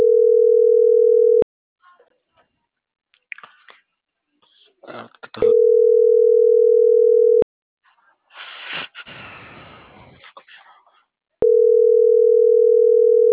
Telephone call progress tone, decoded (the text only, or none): ringback tone